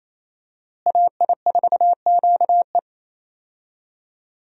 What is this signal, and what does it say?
Morse code, heard as a keyed tone, about 27 wpm, 700 Hz. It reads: AI4QE
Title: USA Call Signs - 5 Characters - Repeated 3x - 28wpm